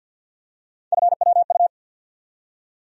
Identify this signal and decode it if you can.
Morse code, heard as a keyed tone, about 52 wpm, 700 Hz. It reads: FWU
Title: USA Call Sign Postfixes - 3 Letters 50wpm